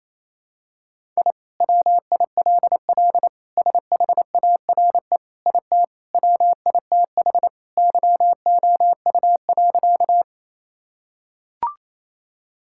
Morse code, heard as a keyed tone, about 28 wpm, 700 Hz. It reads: I WILL SHARE IT WITH YOU.
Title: Sentences from Top 1000 Words - Review Part 6 - 28wpm